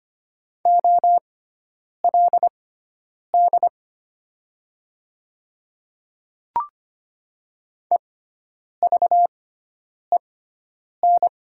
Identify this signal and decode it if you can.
Morse code, heard as a keyed tone, about 25 wpm, spaced as for 9 wpm, 690 Hz. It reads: OLD EVEN
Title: 6x Character Spacing - Top 200 Words - 25wpm